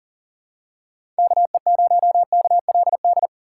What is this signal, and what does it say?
Morse code, heard as a keyed tone, about 40 wpm, 700 Hz. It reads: KE0KLD
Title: USA Call Signs - 6 Characters - 4x Character Spacing - 40wpm